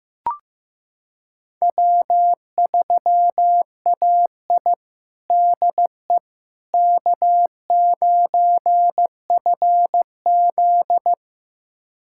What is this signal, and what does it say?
Morse code, heard as a keyed tone, about 15 wpm, 700 Hz. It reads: W3AI DE K9FZ
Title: USA Callsign DE Callsign - 4 Characters - 15wpm